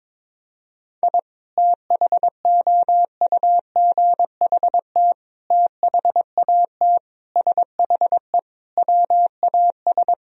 Morse code, heard as a keyed tone, about 22 wpm, 700 Hz. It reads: I THOUGHT THAT SHE WAS
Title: Sentences from Top 1000 Words - Review Part 3 - 22wpm